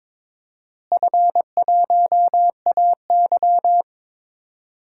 Morse code, heard as a keyed tone, about 21 wpm, 700 Hz. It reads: F1AY
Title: DXCC-100 Call Signs - 4 Characters - Repeated 3x - 22wpm